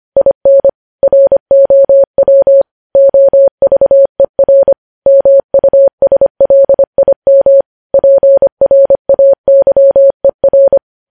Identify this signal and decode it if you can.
Morse code, heard as a keyed tone, about 25 wpm, 550 Hz. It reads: IN ROW OVER MUSLIM PRAYER
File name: podcast / News headlines-1x-25x25